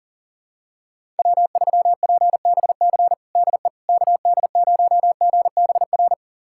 Morse code, heard as a keyed tone, about 40 wpm, 700 Hz. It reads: W3PBC DE KD0GBR
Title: USA Callsign DE Callsign 40wpm